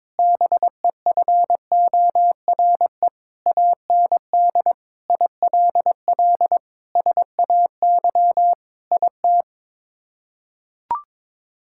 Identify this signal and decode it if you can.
Morse code, heard as a keyed tone, about 22 wpm, 700 Hz. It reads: BEFORE AND ILL SAY IT
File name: practice / 22wpm / Sets of 8 Words - 22wpm